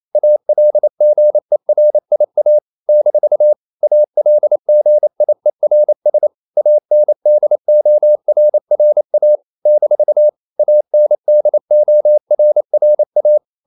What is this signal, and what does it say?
Morse code, heard as a keyed tone, about 28 wpm, 600 Hz. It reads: ALGERIA - ALGIERS ANDORRA - ANDORRA
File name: CW / Practice / NationalCapitals28